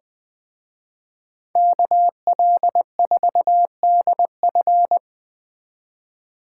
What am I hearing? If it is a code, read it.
Morse code, heard as a keyed tone, about 20 wpm, 700 Hz. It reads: KL4DF